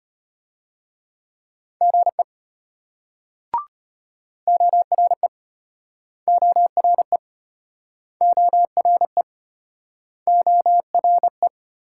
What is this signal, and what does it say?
Morse code, heard as a keyed tone, about 33 wpm, 700 Hz. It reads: GE ORE ORE ORE ORE